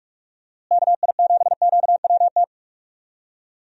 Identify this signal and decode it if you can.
Morse code, heard as a keyed tone, about 46 wpm, 700 Hz. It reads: KI7QWT
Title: USA Call Signs - 6 Characters - Repeated 2x - 45wpm